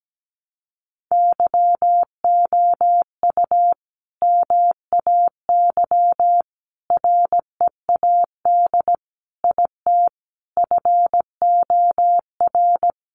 Morse code, heard as a keyed tone, about 17 wpm, 700 Hz. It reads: YOU MAY READ IT FOR
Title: Sentences from Top 600 Words 17wpm